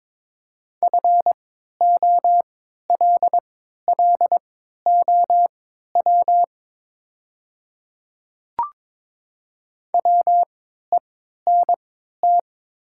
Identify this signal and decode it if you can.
Morse code, heard as a keyed tone, about 22 wpm, spaced as for 12 wpm, 700 Hz. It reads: FOLLOW WENT
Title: 3x Character Spacing - Top 200 Words - 22wpm